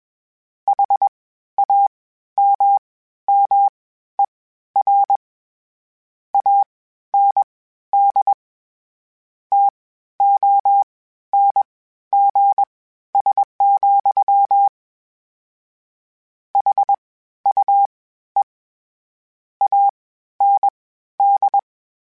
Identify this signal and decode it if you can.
Morse code, heard as a keyed tone, about 21 wpm, spaced as for 12 wpm, 800 Hz. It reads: HAMMER AND TONGS, HUE AND
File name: Misc / Misc-mp3s / WordPairs sp @20